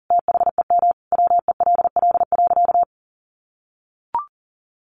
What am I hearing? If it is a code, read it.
Morse code, heard as a keyed tone, about 40 wpm, 700 Hz. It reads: THEM WELL.
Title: Sentences from Top 300 Words 40wpm